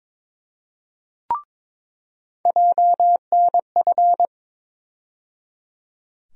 Morse code, heard as a keyed tone, about 22 wpm, 700 Hz. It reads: JNF
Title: USA Call Sign Postfixes - 3 Letters 22wpm